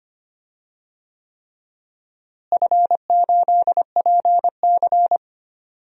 Morse code, heard as a keyed tone, about 25 wpm, 700 Hz. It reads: F8PC
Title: DXCC-100 Call Signs - 4 Characters - Repeated 3x - 25wpm